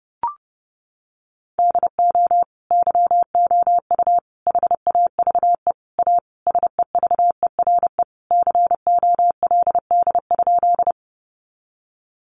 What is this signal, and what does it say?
Morse code, heard as a keyed tone, about 30 wpm, 700 Hz. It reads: DO YOU HAVE A SEVERE COLD?
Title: Sentences from Top 1400 Words 30wpm